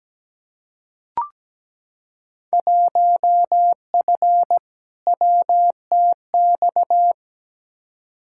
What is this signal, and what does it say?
Morse code, heard as a keyed tone, about 17 wpm, 700 Hz. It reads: 1F WTX
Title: Field Day Exchanges - 17wpm